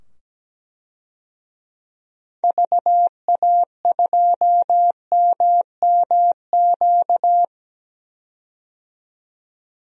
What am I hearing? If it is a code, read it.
Morse code, heard as a keyed tone, about 17 wpm, 700 Hz. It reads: VA2MMQ
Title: DXCC-100 Call Signs - 6 Characters - 17wpm